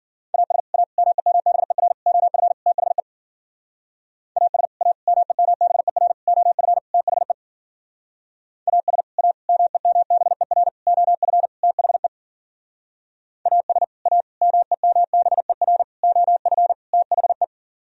Morse code, heard as a keyed tone, about 52 wpm, 700 Hz. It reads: AS A MEMBER OF THE AS A MEMBER OF THE AS A MEMBER OF THE AS A MEMBER OF THE